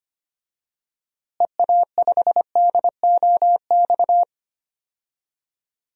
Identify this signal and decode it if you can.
Morse code, heard as a keyed tone, about 25 wpm, 700 Hz. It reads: EA5DOX
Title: DXCC-100 Call Signs - 6 Characters - Repeated 2x - 25wpm